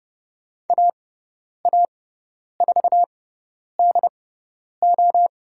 Morse code, heard as a keyed tone, about 30 wpm, spaced as for 10 wpm, 700 Hz. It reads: AA4DO
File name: Lesson 17g - Letter u - Call Signs 30wpm